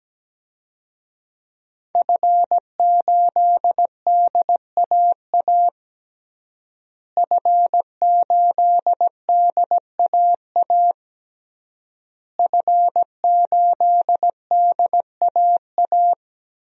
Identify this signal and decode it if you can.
Morse code, heard as a keyed tone, about 17 wpm, 700 Hz. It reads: F8DAA F8DAA F8DAA
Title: DXCC-100 Call Signs - 5 Characters - Repeated 3x - 17wpm